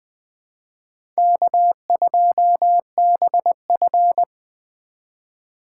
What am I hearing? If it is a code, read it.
Morse code, heard as a keyed tone, about 21 wpm, 700 Hz. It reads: K2BF
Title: USA Call Signs - 3x Character Spacing - 20wpm